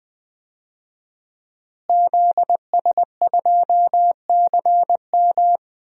Morse code, heard as a keyed tone, about 20 wpm, 700 Hz. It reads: ZS2CM
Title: DXCC-100 Call Signs - 5 Characters - Repeated 2x - 20wpm